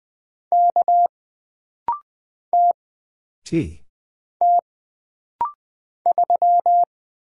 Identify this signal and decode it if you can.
Morse code, heard as a keyed tone, about 20 wpm, 700 Hz. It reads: K T T 3
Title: Single Character (Letter:Number:Punctuation:Prosign) - Rapid-Fire - 20wpm